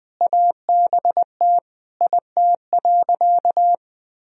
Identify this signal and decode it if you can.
Morse code, heard as a keyed tone, about 21 wpm, 700 Hz. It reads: ABT IT.
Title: Sentences from Top 200 Words and Abbreviations 20wpm